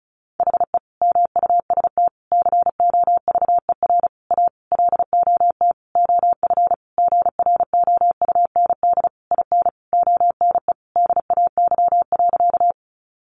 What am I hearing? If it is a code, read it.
Morse code, heard as a keyed tone, about 34 wpm, 700 Hz. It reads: HE MUST COVER A LOT OF GROUND IN ONE DAY.
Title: Sentences from Top 500 Words 35wpm